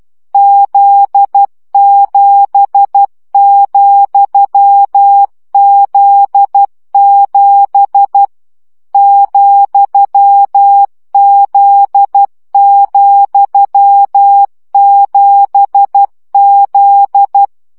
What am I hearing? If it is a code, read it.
Morse code, heard as a keyed tone, about 12 wpm, 800 Hz. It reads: Z7,Z7 ,Z,7Z
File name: CW gyakorlas z 7 vesszo 10 baud